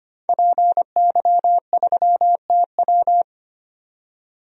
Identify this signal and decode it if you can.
Morse code, heard as a keyed tone, about 25 wpm, 700 Hz. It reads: PY3TW
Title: DXCC-100 Call Signs - 5 Characters - 25wpm